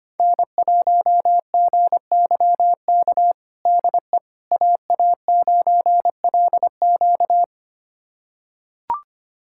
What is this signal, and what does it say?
Morse code, heard as a keyed tone, about 25 wpm, 700 Hz. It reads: N1GYK DE AA9LQ
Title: USA Callsign DE Callsign - 5 Characters - 25wpm